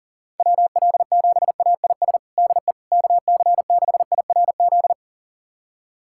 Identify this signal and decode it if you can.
Morse code, heard as a keyed tone, about 40 wpm, 700 Hz. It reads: WL7AIS DE KC6IRZ
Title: USA Callsign DE Callsign 40wpm